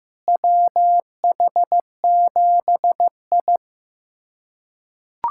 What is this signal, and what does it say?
Morse code, heard as a keyed tone, about 16 wpm, 700 Hz. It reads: WH7I
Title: USA Callsign DE Callsign - 4 Characters - 15wpm